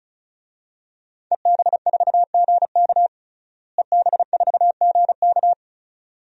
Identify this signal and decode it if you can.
Morse code, heard as a keyed tone, about 36 wpm, 700 Hz. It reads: EB4GK EB4GK
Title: DXCC-100 Call Signs - 5 Characters - Repeated 2x - 35wpm